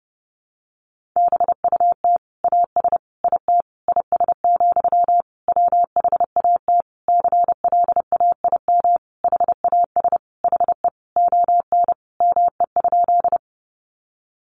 Morse code, heard as a keyed tone, about 30 wpm, 700 Hz. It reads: BUT AS IT IS, WHAT CLAIM HAS HE ON ME?
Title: Sentences from Top 1500 Words 30wpm